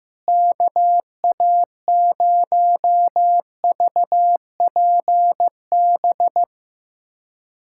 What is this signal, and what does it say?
Morse code, heard as a keyed tone, about 15 wpm, 700 Hz. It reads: KA0VPB
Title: USA Callsign DE Callsign - 6 Characters - 15wpm